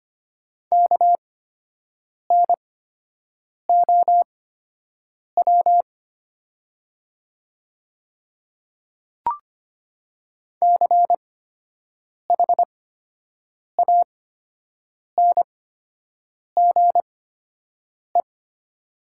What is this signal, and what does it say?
Morse code, heard as a keyed tone, about 25 wpm, spaced as for 7 wpm, 700 Hz. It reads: KNOW CHANGE
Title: 8x Character Spacing - Top 200 Words - 25wpm